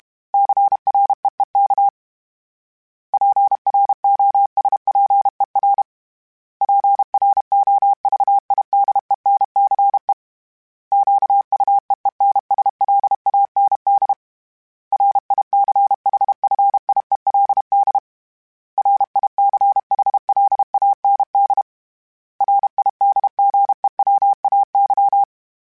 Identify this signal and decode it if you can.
Morse code, heard as a keyed tone, about 32 wpm, 800 Hz. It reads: CREEK PROSPER PROVIDENCE QUEENSLAND RICHFIELD RICHLAND RIDGEWAY
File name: fromSMs / FromSMs mp3s / Cities@30